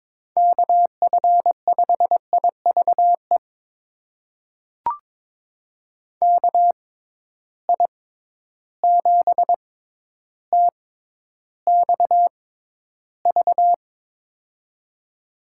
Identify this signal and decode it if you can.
Morse code, heard as a keyed tone, about 21 wpm, spaced as for 11 wpm, 700 Hz. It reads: KF5IVE KI7TXV
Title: USA Call Signs - 6 Characters - 6x Character Spacing - 22wpm